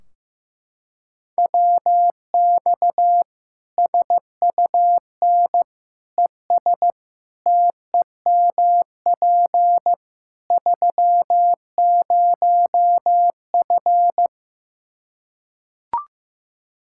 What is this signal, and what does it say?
Morse code, heard as a keyed tone, about 15 wpm, 700 Hz. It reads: WX SUN ES TEMP 30F
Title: Practice QSO - Exchange 2 - Info - 15wpm